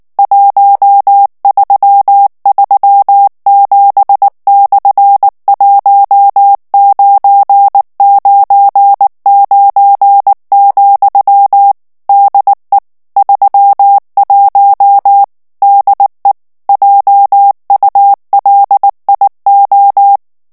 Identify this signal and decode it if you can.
Morse code, heard as a keyed tone, about 19 wpm, 800 Hz. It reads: 1337/1999, DE 31 DE JULIO
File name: morse mp3 / ESPAGNOL TEXTE EN CLAIR 2 VITESSE 18 MOTS MINUTES